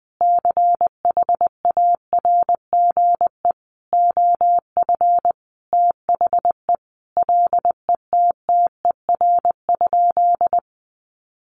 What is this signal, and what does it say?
Morse code, heard as a keyed tone, about 19 wpm, 700 Hz. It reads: CHARGE OF THE LETTER?
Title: Sentences from Top 900 Words 20wpm